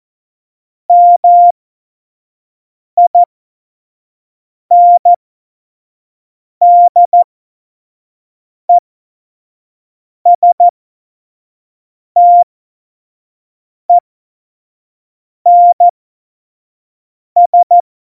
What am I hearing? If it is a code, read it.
Morse code, heard as a keyed tone, about 14 wpm, spaced as for 5 wpm, 700 Hz. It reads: MINDESTENS